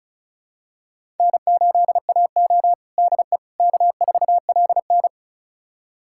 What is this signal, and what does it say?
Morse code, heard as a keyed tone, about 34 wpm, 700 Hz. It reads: N8AO DE K4LN
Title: USA Callsign DE Callsign - 4 Characters - 35wpm